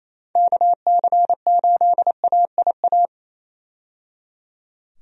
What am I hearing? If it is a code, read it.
Morse code, heard as a keyed tone, about 27 wpm, 700 Hz. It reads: KC8AIA
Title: USA Call Signs - 6 Characters - 28wpm